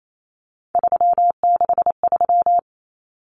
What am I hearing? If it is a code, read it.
Morse code, heard as a keyed tone, about 28 wpm, 700 Hz. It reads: 363
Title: Common Cut Numbers - 3-Digit 28wpm